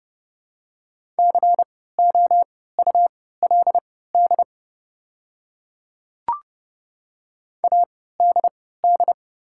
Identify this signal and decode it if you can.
Morse code, heard as a keyed tone, about 30 wpm, spaced as for 17 wpm, 700 Hz. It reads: COULD ADD
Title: 3x Character Spacing - Top 300 Words - 30wpm